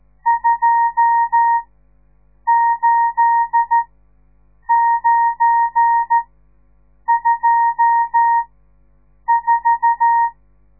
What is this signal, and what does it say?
Morse code, heard as a keyed tone, about 13 wpm, spaced as for 7 wpm, 930 Hz. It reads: 28924
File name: radiomorningstar / morse / les 10